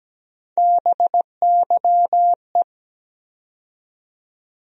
Morse code, heard as a keyed tone, about 17 wpm, 700 Hz. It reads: BYE
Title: USA Call Sign Postfixes - 3 Letters 17wpm